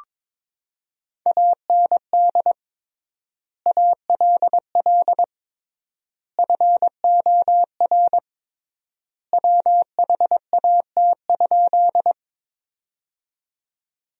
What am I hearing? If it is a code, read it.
Morse code, heard as a keyed tone, about 22 wpm, 700 Hz. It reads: AND ALL FOR WHAT?
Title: 3x Word-Spacing - Sentences from Top 100 Words - 22wpm